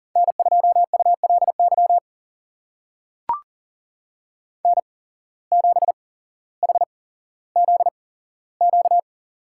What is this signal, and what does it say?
Morse code, heard as a keyed tone, about 37 wpm, spaced as for 17 wpm, 700 Hz. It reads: N2ULY N7HZQ
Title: USA Call Signs - 5 Characters - 8x Character Spacing - 40wpm